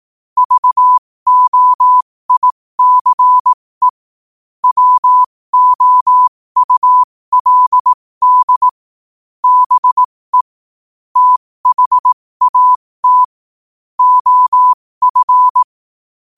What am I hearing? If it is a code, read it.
Morse code, heard as a keyed tone, about 18 wpm, spaced as for 15 wpm, 1000 Hz. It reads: VOICE WOULD BE THAT OF
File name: qotdfiles / 15wpm